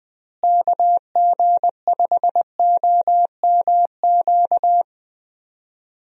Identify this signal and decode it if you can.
Morse code, heard as a keyed tone, about 20 wpm, 700 Hz. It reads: KG5OMQ
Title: USA Call Signs - 6 Characters - Repeated 2x - 20wpm